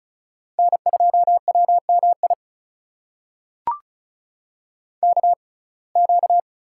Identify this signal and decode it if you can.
Morse code, heard as a keyed tone, about 36 wpm, spaced as for 25 wpm, 700 Hz. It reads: N2WMI KQ
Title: USA Call Signs - 5 Characters - 6x Character Spacing - 35wpm